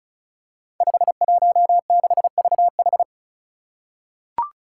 Morse code, heard as a keyed tone, about 36 wpm, 700 Hz. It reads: 516VH
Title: Five Challenging Letter-Number Combinations 35wpm